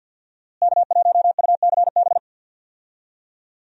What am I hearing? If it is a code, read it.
Morse code, heard as a keyed tone, about 46 wpm, 700 Hz. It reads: K1UCB
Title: USA Call Signs - 5 Characters - Repeated 2x - 50wpm